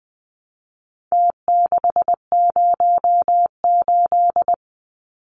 Morse code, heard as a keyed tone, about 20 wpm, 700 Hz. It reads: T608
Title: Numbers - Contest Style - 4-Digit 20wpm